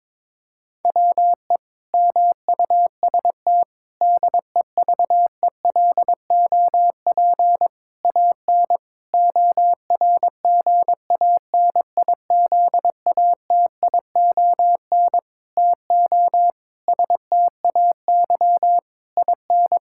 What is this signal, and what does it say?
Morse code, heard as a keyed tone, about 22 wpm, 700 Hz. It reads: WE MUST DEVELOP AN ORGANIZATION TO STAY IN
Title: Sentences from Top 1000 Words - Review Part 5 - 22wpm